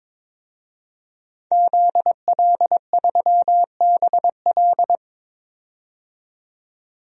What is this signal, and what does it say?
Morse code, heard as a keyed tone, about 22 wpm, 700 Hz. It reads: ZL3BL
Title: DXCC-100 Call Signs - 5 Characters - 22wpm